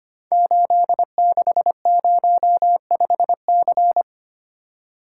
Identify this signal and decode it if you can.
Morse code, heard as a keyed tone, about 25 wpm, 700 Hz. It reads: 8605C
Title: Code Groups - 5 Letter:Numbers - 25wpm